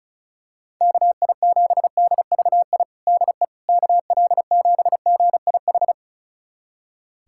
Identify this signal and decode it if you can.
Morse code, heard as a keyed tone, about 34 wpm, 700 Hz. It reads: KI7DVI DE KL7GIH
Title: USA Callsign DE Callsign - 6 Characters - 35wpm